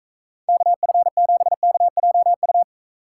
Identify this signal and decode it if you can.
Morse code, heard as a keyed tone, about 44 wpm, 700 Hz. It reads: KF7KJU